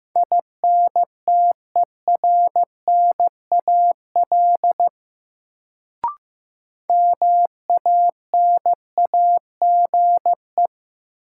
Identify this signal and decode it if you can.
Morse code, heard as a keyed tone, about 15 wpm, 700 Hz. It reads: INTERNAL MANAGE
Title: Sentences from Top 1400 Words - New Word Drill - 15wpm